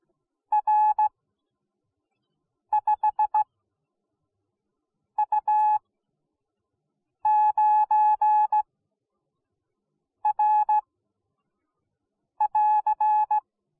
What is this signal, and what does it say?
Morse code, heard as a keyed tone, about 17 wpm, spaced as for 5 wpm, 850 Hz. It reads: R5U9R <AR>